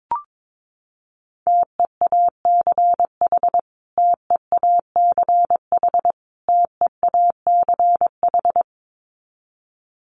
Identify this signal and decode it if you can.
Morse code, heard as a keyed tone, about 22 wpm, 700 Hz. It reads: TEACH TEACH TEACH
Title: Top 500 Words Repeated 22wpm